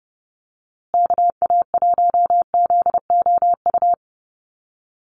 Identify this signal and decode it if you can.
Morse code, heard as a keyed tone, about 30 wpm, 700 Hz. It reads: KA1ZOU
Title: USA Call Signs Repeated 2x 30wpm